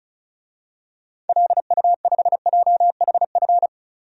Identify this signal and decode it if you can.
Morse code, heard as a keyed tone, about 34 wpm, 700 Hz. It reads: LU5JHF
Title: DXCC-100 Call Signs - 6 Characters - 35wpm